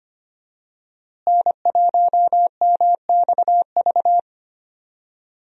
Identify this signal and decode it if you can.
Morse code, heard as a keyed tone, about 25 wpm, 700 Hz. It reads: N1MXV
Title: USA Call Signs - 5 Characters - Repeated 3x - 25wpm